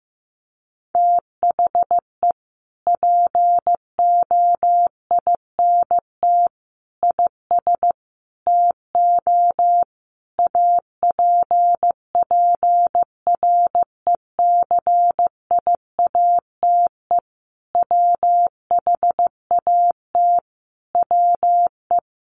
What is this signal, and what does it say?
Morse code, heard as a keyed tone, about 15 wpm, 700 Hz. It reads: THE POINT IS TO APPRECIATE WHAT WE
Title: Sentences from Top 1300 Words 15wpm